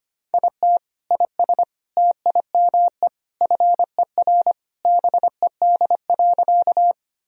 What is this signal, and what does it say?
Morse code, heard as a keyed tone, about 25 wpm, 700 Hz. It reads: IT IS TIME FER BED.
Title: Sentences from Top 500 Words and Abbreviations 25wpm